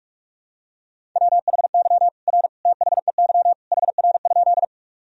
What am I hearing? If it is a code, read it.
Morse code, heard as a keyed tone, about 43 wpm, 700 Hz. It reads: WHY R THEY HR?
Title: Sentences from Top 200 Words and Abbreviations 45wpm